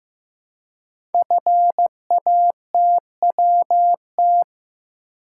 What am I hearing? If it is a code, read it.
Morse code, heard as a keyed tone, about 15 wpm, 700 Hz. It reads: FATWT
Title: Code Groups - 5 Letters - 15wpm